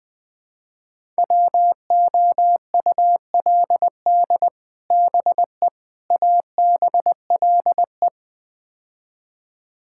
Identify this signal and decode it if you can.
Morse code, heard as a keyed tone, about 20 wpm, 700 Hz. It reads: WOULD BE ABLE
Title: Build A Phrase - 4-Word Phrases - 20wpm